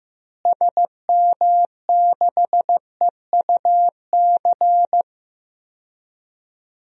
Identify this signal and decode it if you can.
Morse code, heard as a keyed tone, about 15 wpm, 700 Hz. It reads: SM6EUC
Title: DXCC-100 Call Signs - 6 Characters - Repeated 2x - 15wpm